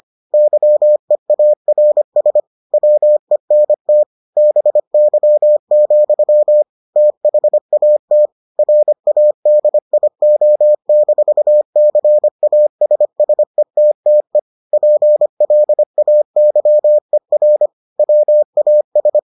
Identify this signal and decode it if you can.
Morse code, heard as a keyed tone, about 25 wpm, 600 Hz. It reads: YEARS WENT BY, THAT RADIO-CASSETTE PLAYER WAS